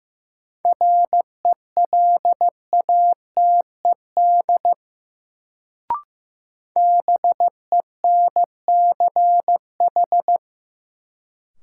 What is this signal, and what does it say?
Morse code, heard as a keyed tone, about 15 wpm, 700 Hz. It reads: RELATED BENCH
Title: Sentences from Top 1400 Words - New Word Drill Alt - 15wpm